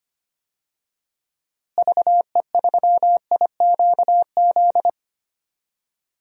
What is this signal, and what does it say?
Morse code, heard as a keyed tone, about 25 wpm, 700 Hz. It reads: VE3IQZ
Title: DXCC-100 Call Signs - 6 Characters - Repeated 3x - 25wpm